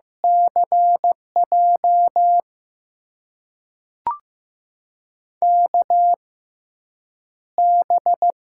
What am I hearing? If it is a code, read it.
Morse code, heard as a keyed tone, about 15 wpm, spaced as for 8 wpm, 700 Hz. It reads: CJ KB
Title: USA Call Signs - 6 Characters - 6x Character Spacing - 15wpm